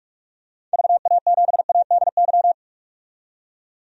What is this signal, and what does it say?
Morse code, heard as a keyed tone, about 43 wpm, 700 Hz. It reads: VA7ADY
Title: DXCC-100 Call Signs - 6 Characters - 45wpm